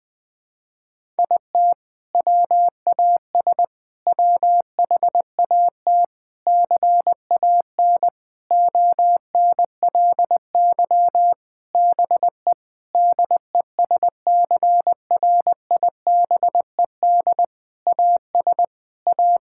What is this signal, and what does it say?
Morse code, heard as a keyed tone, about 21 wpm, 700 Hz. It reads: IT WAS WHAT CAN ONLY BE DESCRIBED AS A
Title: Sentences with 7-Word Phrases - Spoken Beginning - 20wpm